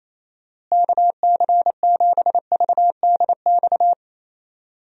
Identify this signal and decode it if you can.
Morse code, heard as a keyed tone, about 28 wpm, 700 Hz. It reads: KC7VDX
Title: USA Call Signs - 6 Characters - Repeated 3x - 28wpm